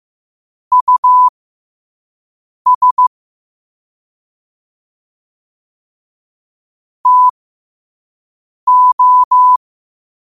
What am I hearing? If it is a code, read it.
Morse code, heard as a keyed tone, about 15 wpm, spaced as for 5 wpm, 1000 Hz. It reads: US TO
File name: qotdfiles / 5wpm